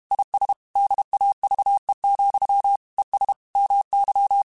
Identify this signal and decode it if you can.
Morse code, heard as a keyed tone, about 32 wpm, 800 Hz. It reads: IS DAVE, ES MY
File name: QSOs / QSOs-mp3s / WD0BGP 1@30